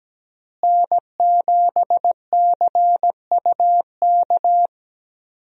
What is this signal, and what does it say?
Morse code, heard as a keyed tone, about 17 wpm, 700 Hz. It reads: N7CUK
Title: USA Call Signs - 5 Characters - Repeated 2x - 17wpm